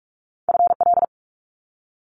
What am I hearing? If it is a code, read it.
Morse code, heard as a keyed tone, about 46 wpm, 700 Hz. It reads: FL
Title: Multi-letter Phonemes Quick Fire 45wpm